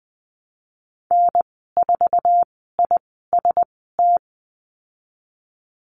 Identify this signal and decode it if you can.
Morse code, heard as a keyed tone, about 20 wpm, spaced as for 15 wpm, 700 Hz. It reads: N4IST
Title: 055 Lesson 10g - Letter r - Call Signs 20wpm